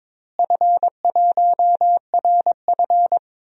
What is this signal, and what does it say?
Morse code, heard as a keyed tone, about 22 wpm, 700 Hz. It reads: F1RF
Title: DXCC-100 Call Signs - 4 Characters - Repeated 2x - 22wpm